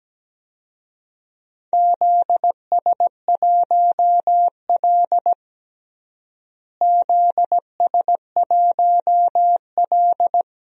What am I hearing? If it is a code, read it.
Morse code, heard as a keyed tone, about 17 wpm, 700 Hz. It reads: ZS1L ZS1L
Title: DXCC-100 Call Signs - 4 Characters - Repeated 3x - 17wpm